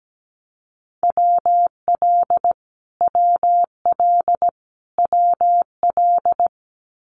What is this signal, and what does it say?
Morse code, heard as a keyed tone, about 17 wpm, 700 Hz. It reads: WL WL WL
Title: Abbreviations Repeated 3x - 17wpm